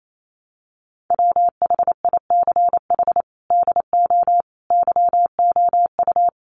Morse code, heard as a keyed tone, about 28 wpm, 700 Hz. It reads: WHICH DO YOU
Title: Sentences from Top 1100 Words 28wpm